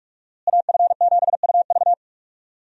Morse code, heard as a keyed tone, about 46 wpm, 700 Hz. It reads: AF7UV
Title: USA Call Signs - 5 Characters - Repeated 2x - 45wpm